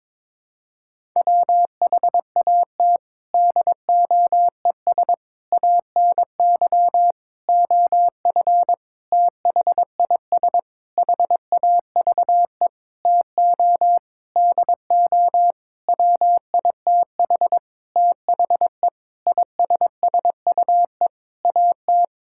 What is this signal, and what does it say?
Morse code, heard as a keyed tone, about 22 wpm, 700 Hz. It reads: WHAT DOES ANY OF THIS HAVE TO DO WITH THE ISSUE AT
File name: Sentences with 9-Word Phrases - 22wpm